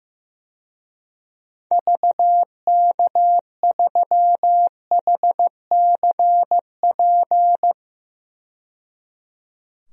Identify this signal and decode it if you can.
Morse code, heard as a keyed tone, about 15 wpm, 700 Hz. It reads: VK3HCP
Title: DXCC-100 Call Signs - 6 Characters - 15wpm